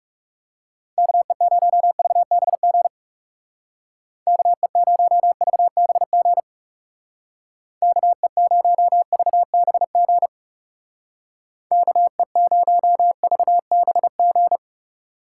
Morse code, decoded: KE0VBG KE0VBG KE0VBG KE0VBG

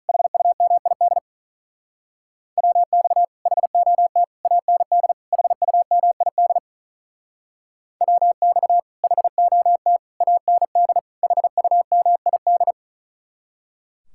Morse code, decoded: HUMID WX HOT AND HUMID WX HOT AND HUMID

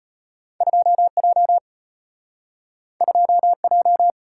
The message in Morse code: 2J 2J

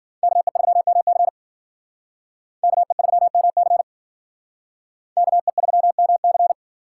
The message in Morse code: KE3MC KE3MC KE3MC

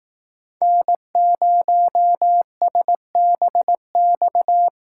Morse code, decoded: N0SBX

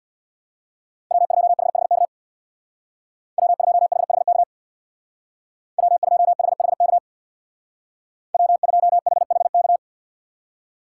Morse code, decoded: W2HHK W2HHK W2HHK W2HHK

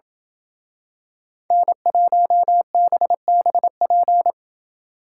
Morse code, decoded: N1BBP

